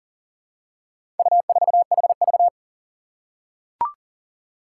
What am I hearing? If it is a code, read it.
Morse code, heard as a keyed tone, about 40 wpm, 700 Hz. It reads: U4HV
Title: Four Challenging Letter-Number Combinations 40wpm